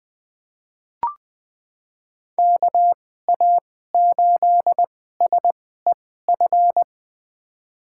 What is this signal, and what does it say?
Morse code, heard as a keyed tone, about 20 wpm, spaced as for 14 wpm, 700 Hz. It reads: KA8SEF
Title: USA Call Signs - 2x Character Spacing - 20wpm